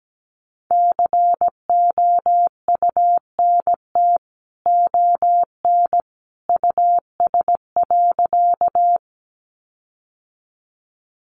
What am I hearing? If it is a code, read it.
Morse code, heard as a keyed tone, about 17 wpm, 700 Hz. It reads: COUNT ON US.
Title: Sentences from Top 1500 Words 17wpm